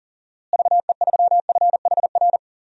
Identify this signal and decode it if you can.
Morse code, heard as a keyed tone, about 39 wpm, 700 Hz. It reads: VE3FHR
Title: DXCC-100 Call Signs - 6 Characters - Repeated 2x - 40wpm